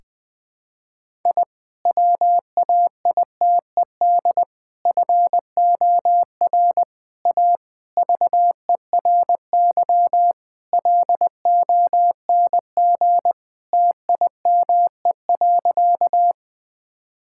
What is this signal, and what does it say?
Morse code, heard as a keyed tone, about 20 wpm, 700 Hz. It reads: I WAITED FOR A VERY LONG TIME.
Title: Sentences with 5-Word Phrases - Spoken Beginning - 20wpm